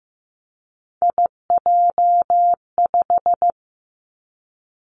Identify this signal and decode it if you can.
Morse code, heard as a keyed tone, about 15 wpm, 700 Hz. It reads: IJ5
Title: Three Challenging Letter-Number Combinations 15wpm